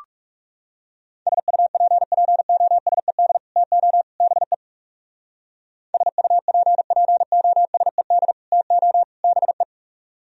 Morse code, heard as a keyed tone, about 39 wpm, 700 Hz. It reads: SUPPOSED TO BE SUPPOSED TO BE